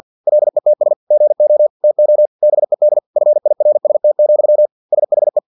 Morse code, heard as a keyed tone, about 50 wpm, 600 Hz. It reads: LETS GO TO BED FIRST, SHE
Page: Never 49wpm